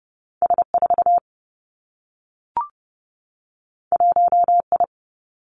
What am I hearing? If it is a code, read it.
Morse code, heard as a keyed tone, about 30 wpm, 700 Hz. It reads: S 4 1 I
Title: Two Challenging Letter-Number Combinations 30wpm